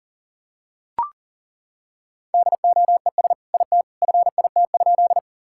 Morse code, heard as a keyed tone, about 40 wpm, 700 Hz. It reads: DOES IT FIT?
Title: Sentences from Top 1000 Words - Review Part 5 - 40wpm